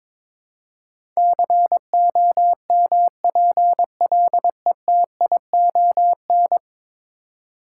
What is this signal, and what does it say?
Morse code, heard as a keyed tone, about 21 wpm, 700 Hz. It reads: COMPLETION
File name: practice / 22wpm / 10 Letter Words - First 3 Letters Given - 22wpm